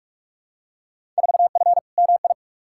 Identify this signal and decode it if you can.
Morse code, heard as a keyed tone, about 46 wpm, 700 Hz. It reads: 4F MI
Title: Field Day Exchanges - 45wpm